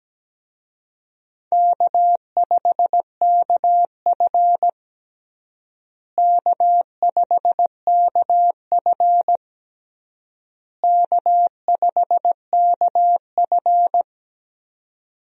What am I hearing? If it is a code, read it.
Morse code, heard as a keyed tone, about 17 wpm, 700 Hz. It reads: K5KF K5KF K5KF
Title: USA Call Signs - 4 Characters - Repeated 3x - 17wpm